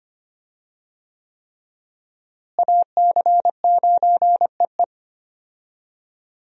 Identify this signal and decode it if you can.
Morse code, heard as a keyed tone, about 25 wpm, 700 Hz. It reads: AC9EE
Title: USA Call Signs - 5 Characters - Repeated 3x - 25wpm